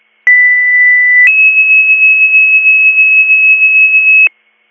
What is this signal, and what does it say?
Two-tone page, heard as a one-second tone, then a three-second tone, about 2000 then 2600 Hz. It is heard from 0.3 to 4.3 s.